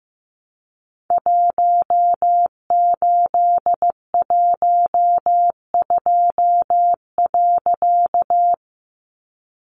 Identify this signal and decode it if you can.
Morse code, heard as a keyed tone, about 15 wpm, 700 Hz. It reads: 1812.
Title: A Sound Of Thunder 15wpm